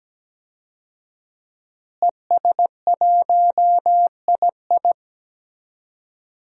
Morse code, heard as a keyed tone, about 17 wpm, 700 Hz. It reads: ES1II
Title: DXCC-100 Call Signs - 5 Characters - Repeated 3x - 17wpm